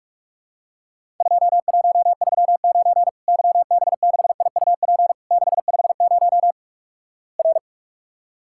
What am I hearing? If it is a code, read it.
Morse code, heard as a keyed tone, about 46 wpm, 700 Hz. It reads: 2139 YB6IUP 650 R
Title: Worked All Europe DX Contest - QTC Exchange - 45wpm